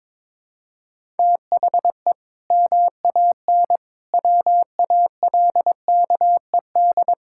Morse code, heard as a keyed tone, about 22 wpm, 700 Hz. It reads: THE MAN WALKED
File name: practice / 22wpm / Sentences with 2-Word Phrases - 22wpm